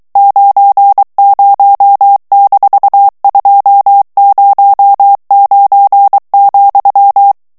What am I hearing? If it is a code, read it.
Morse code, heard as a keyed tone, about 23 wpm, 800 Hz. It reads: 90-209,
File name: ANGLAIS texte clair 3 vitesse 22 mots minute